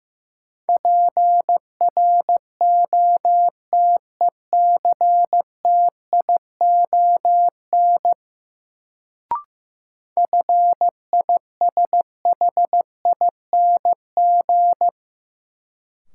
Morse code, heard as a keyed tone, about 15 wpm, 700 Hz. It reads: PROTECTION FISHING
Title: Sentences from Top 1600 Words - New Word Drill Alt - 15wpm